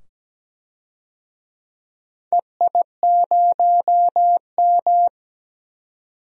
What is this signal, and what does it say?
Morse code, heard as a keyed tone, about 17 wpm, 700 Hz. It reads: EI0M
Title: DXCC-100 Call Signs - 4 Characters - 17wpm